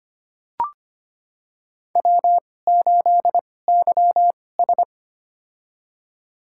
Morse code, heard as a keyed tone, about 25 wpm, spaced as for 18 wpm, 700 Hz. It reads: W8YS
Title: USA Call Signs - 4 Characters - 2x Character Spacing - 25wpm